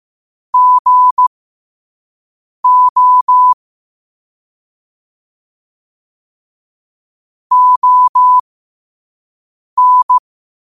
Morse code, heard as a keyed tone, about 15 wpm, spaced as for 5 wpm, 1000 Hz. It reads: GO ON
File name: qotdfiles / 5wpm